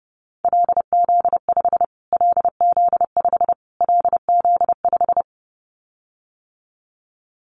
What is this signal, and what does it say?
Morse code, heard as a keyed tone, about 30 wpm, 690 Hz. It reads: LZ5 LZ5 LZ5
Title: Callsign Prefixes for Top 25 DXCC - Repeated 3x - 30wpm